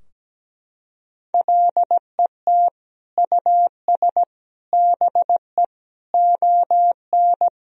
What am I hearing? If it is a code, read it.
Morse code, heard as a keyed tone, about 17 wpm, 700 Hz. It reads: LET US BE ON
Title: Sentences from Top 1000 Words - Review Part 4 - 17wpm